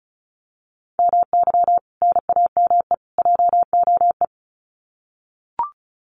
Morse code, heard as a keyed tone, about 34 wpm, 700 Hz. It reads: MY NAME JOE
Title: Common QSO Phrases 35wpm